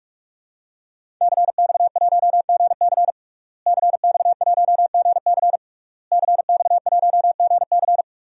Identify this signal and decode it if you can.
Morse code, heard as a keyed tone, about 46 wpm, 700 Hz. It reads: CX1GC CX1GC CX1GC
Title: DXCC-100 Call Signs - 5 Characters - Repeated 3x - 45wpm